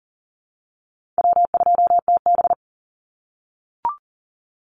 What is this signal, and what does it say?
Morse code, heard as a keyed tone, about 40 wpm, 700 Hz. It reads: W2TB
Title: Easy Call Signs - 40wpm